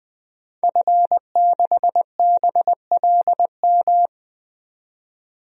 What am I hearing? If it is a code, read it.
Morse code, heard as a keyed tone, about 20 wpm, 700 Hz. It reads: F6BLM
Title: DXCC-100 Call Signs - 5 Characters - 20wpm